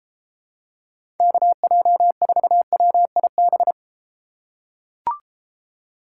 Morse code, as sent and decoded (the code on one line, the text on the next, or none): -.- .--- ....- .-- .. -...
KJ4WIB